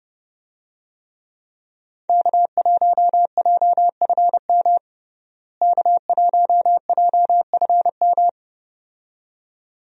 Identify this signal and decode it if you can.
Morse code, heard as a keyed tone, about 32 wpm, 700 Hz. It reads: K1JFM K1JFM
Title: USA Call Signs - 5 Characters - Repeated 2x - 30wpm